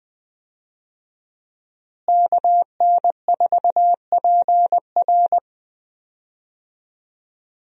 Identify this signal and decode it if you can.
Morse code, heard as a keyed tone, about 20 wpm, 700 Hz. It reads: KN4PR
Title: USA Call Signs - 5 Characters - Repeated 2x - 20wpm